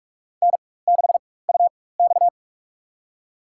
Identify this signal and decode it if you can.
Morse code, heard as a keyed tone, about 46 wpm, spaced as for 21 wpm, 700 Hz. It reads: N6UX
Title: USA Call Signs - 4 Characters - 4x Character Spacing - 45wpm